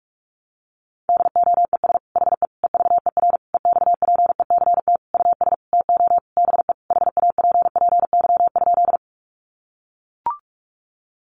Morse code, heard as a keyed tone, about 44 wpm, 700 Hz. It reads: DOES HE EVER EXPECT US TO BE HAPPY?
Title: Sentences from Top 1100 Words 45wpm